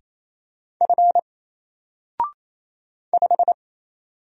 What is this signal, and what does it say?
Morse code, heard as a keyed tone, about 27 wpm, 700 Hz. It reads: F 5
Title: Single Character (Letter:Number:Punctuation:Prosign) 28wpm